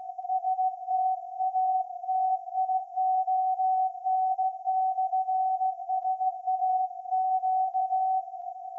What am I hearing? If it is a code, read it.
Morse code, heard as a keyed tone, about 16 wpm, 750 Hz. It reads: STATION/SAQ
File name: iz7slz / VLF / SAQ 30-06-2024 0900